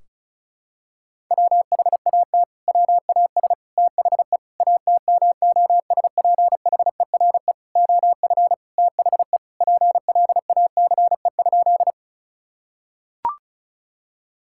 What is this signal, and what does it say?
Morse code, heard as a keyed tone, about 34 wpm, 700 Hz. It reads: WHAT WAS THE ATMOSPHERE OF THE PLACE?
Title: Sentences from Top 1600 Words 35wpm